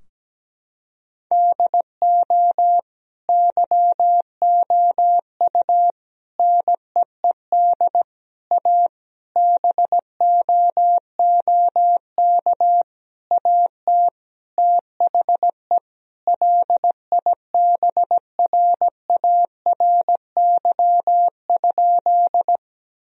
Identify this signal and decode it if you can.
Morse code, heard as a keyed tone, about 17 wpm, 700 Hz. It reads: DO YOU NEED A BOOK AT THE LIBRARY?
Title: Sentences from Top 1000 Words - Review Part 4 - 17wpm